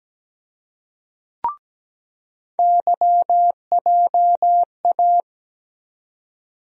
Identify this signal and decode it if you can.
Morse code, heard as a keyed tone, about 17 wpm, 700 Hz. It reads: YJA
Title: USA Call Sign Postfixes - 3 Letters 17wpm